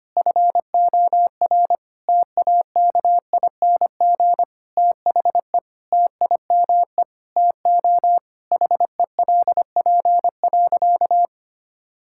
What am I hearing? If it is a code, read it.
Morse code, heard as a keyed tone, about 25 wpm, 700 Hz. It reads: FOR TAKING THE TIME TO HELP.
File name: practice / 25wpm / Sentences with 9-Word Phrases - 25wpm